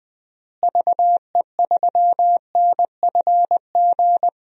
Morse code, heard as a keyed tone, about 20 wpm, 700 Hz. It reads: VE3NFG
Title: DXCC-100 Call Signs - 6 Characters - Repeated 3x - 20wpm